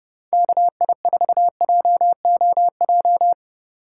Morse code, heard as a keyed tone, about 32 wpm, 700 Hz. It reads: KI4JOJ
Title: USA Call Signs - 6 Characters - 2x Character Spacing - 30wpm